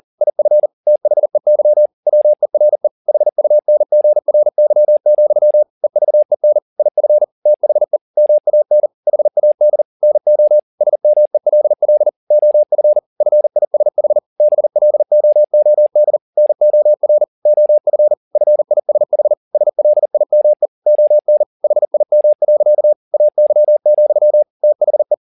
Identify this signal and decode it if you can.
Morse code, heard as a keyed tone, about 40 wpm, 600 Hz. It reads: IF THEY WERE HUNGRY, EVEN IF THE MAN HAD NO SMELL OF FISH BLOOD NOR OF FISH SLIME ON HIM. AY, THE
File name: the old man and the sea 2 40wpm